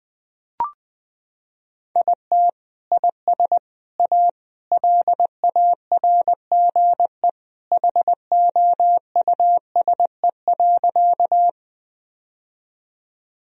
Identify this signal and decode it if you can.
Morse code, heard as a keyed tone, about 21 wpm, 700 Hz. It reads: IT IS A LARGE HOUSE.
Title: Sentences from Top 1000 Words - Review Part 1 - 20wpm